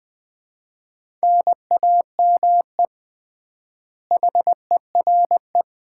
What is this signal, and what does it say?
Morse code, heard as a keyed tone, about 21 wpm, 700 Hz. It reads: NAME HERE
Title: Common QSO Phrases - 3x Word-Spacing - 20wpm